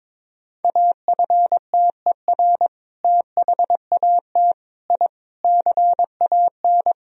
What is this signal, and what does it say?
Morse code, heard as a keyed tone, about 22 wpm, 700 Hz. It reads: AFTER THAT I CAN
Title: Sentences from Top 1000 Words - Review Part 2 - 22wpm